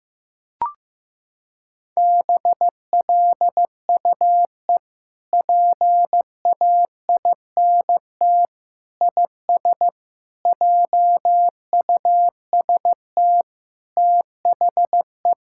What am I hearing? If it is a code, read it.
Morse code, heard as a keyed tone, about 15 wpm, 700 Hz. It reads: BLUE PAINT IS JUST THE
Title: Sentences from Top 1000 Words - Review Part 6 - 15wpm